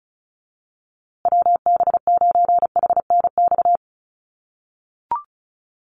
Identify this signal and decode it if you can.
Morse code, heard as a keyed tone, about 34 wpm, 700 Hz. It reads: WB9HNX
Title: USA Call Signs Repeated 3x 35wpm